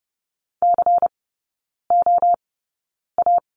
Morse code, heard as a keyed tone, about 30 wpm, spaced as for 9 wpm, 690 Hz. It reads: COA
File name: Lesson 16e - Letter c - Three Characters 30wpm